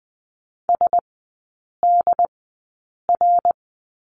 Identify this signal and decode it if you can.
Morse code, heard as a keyed tone, about 20 wpm, spaced as for 8 wpm, 700 Hz. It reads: SDR
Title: 074 Lesson 13e - Letter L - Three Characters 20wpm